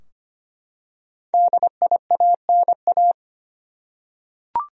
Morse code, heard as a keyed tone, about 25 wpm, 700 Hz. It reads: DIANA
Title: USA Common Names - Max 5 Letters - 25wpm